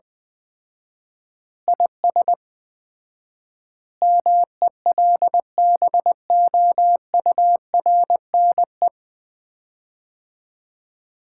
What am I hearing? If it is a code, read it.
Morse code, heard as a keyed tone, about 20 wpm, 700 Hz. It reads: IS MELBOURNE
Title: Common QSO Phrases - 4x Word-Spacing - 20wpm